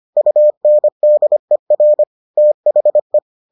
Morse code, heard as a keyed tone, about 25 wpm, 600 Hz. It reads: UNDER THE